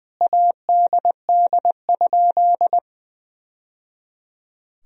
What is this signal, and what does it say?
Morse code, heard as a keyed tone, about 20 wpm, 700 Hz. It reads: ADD?
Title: Sentences from Top 1000 Words - Review Part 1 - 20wpm